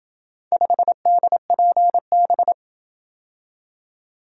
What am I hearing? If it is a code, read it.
Morse code, heard as a keyed tone, about 27 wpm, 700 Hz. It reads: 5DPB